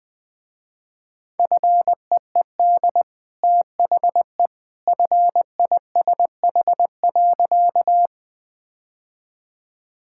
Morse code, heard as a keyed tone, about 21 wpm, 700 Hz. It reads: FEED THE FISH.
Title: Easy Sentences with Hints - 20wpm